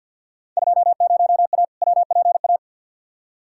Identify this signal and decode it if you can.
Morse code, heard as a keyed tone, about 51 wpm, 700 Hz. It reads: 20A WPA
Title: Field Day Exchanges Encore - 50wpm